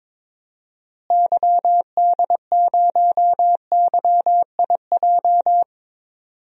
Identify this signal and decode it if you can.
Morse code, heard as a keyed tone, about 22 wpm, 700 Hz. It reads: YD0YIJ
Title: DXCC-100 Call Signs - 6 Characters - Repeated 3x - 22wpm